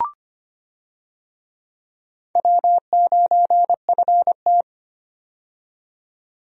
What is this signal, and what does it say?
Morse code, heard as a keyed tone, about 25 wpm, 700 Hz. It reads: W9FT
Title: USA Call Signs - 4 Characters - 25wpm